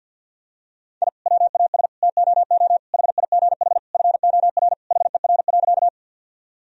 Morse code, heard as a keyed tone, about 51 wpm, 700 Hz. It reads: I WAS TOO HIGH FOR HER.
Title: Sentences from Top 1000 Words - Review Part 1 - 50wpm